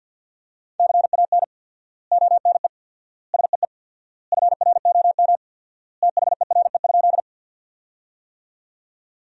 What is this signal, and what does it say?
Morse code, heard as a keyed tone, about 50 wpm, 700 Hz. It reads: CAN ONE SEE FROM THERE?
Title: 4x Word-Spacing - Sentences from Top 100 Words - 50wpm